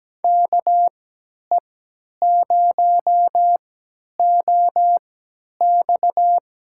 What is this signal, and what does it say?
Morse code, heard as a keyed tone, about 17 wpm, spaced as for 10 wpm, 700 Hz. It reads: KE0OX
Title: USA Call Signs - 3x Character Spacing - 17wpm